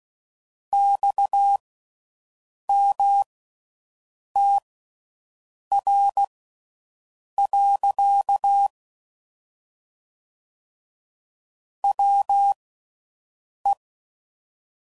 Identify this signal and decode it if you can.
Morse code, heard as a keyed tone, about 16 wpm, spaced as for 6 wpm, 800 Hz. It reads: XMTR. WE